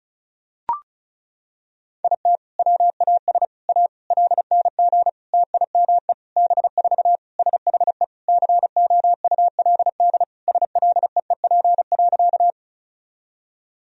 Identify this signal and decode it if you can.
Morse code, heard as a keyed tone, about 36 wpm, 700 Hz. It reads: IT WAS A LNG TIME B4 SHE COULD SLEEP.
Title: Sentences from Top 500 Words and Abbreviations 35wpm